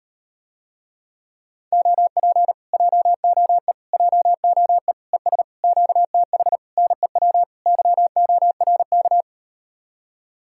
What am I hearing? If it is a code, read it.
Morse code, heard as a keyed tone, about 38 wpm, 700 Hz. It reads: OP JOE JOE ES QTH NEW YORK